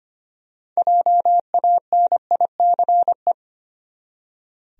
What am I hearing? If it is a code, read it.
Morse code, heard as a keyed tone, about 25 wpm, 700 Hz. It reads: JANICE
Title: USA Common Names - 25wpm